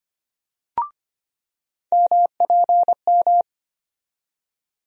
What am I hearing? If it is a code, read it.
Morse code, heard as a keyed tone, about 25 wpm, 700 Hz. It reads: MPM
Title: USA Call Sign Postfixes - 3 Letters 25wpm